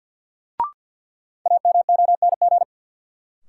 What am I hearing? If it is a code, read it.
Morse code, heard as a keyed tone, about 52 wpm, 700 Hz. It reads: AMONG
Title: Sentences from Top 500 Words - New Word Drill - 50wpm